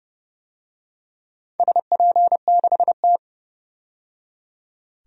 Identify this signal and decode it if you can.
Morse code, heard as a keyed tone, about 29 wpm, 700 Hz. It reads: SP6T
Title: DXCC-100 Call Signs - 4 Characters - 30wpm